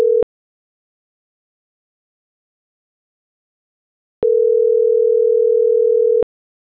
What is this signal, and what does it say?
Ringback tone (North America), 440 + 480 Hz.